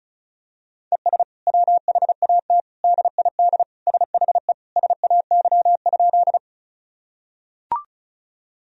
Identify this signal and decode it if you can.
Morse code, heard as a keyed tone, about 34 wpm, 700 Hz. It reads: ES WHAT DID SHE SAY?
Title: Sentences from Top 200 Words and Abbreviations 35wpm